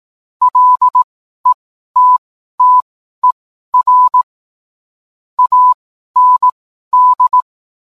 Morse code, heard as a keyed tone, about 18 wpm, spaced as for 12 wpm, 1000 Hz. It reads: LETTER AND
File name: cwblog / cw mp3 directory / mem morse 12